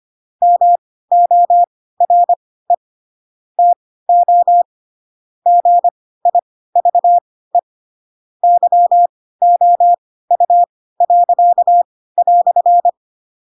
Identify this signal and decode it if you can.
Morse code, heard as a keyed tone, about 25 wpm, spaced as for 16 wpm, 700 Hz. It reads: MORE TO GIVE YOU."
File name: Pinocchio / Pinocchio ch7 10 25 ik1hge